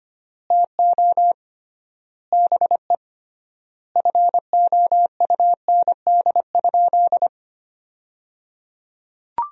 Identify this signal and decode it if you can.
Morse code, heard as a keyed tone, about 25 wpm, 700 Hz. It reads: TO BE FOUND?
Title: 3x Word-Spacing - Sentences from Top 300 Words - 25wpm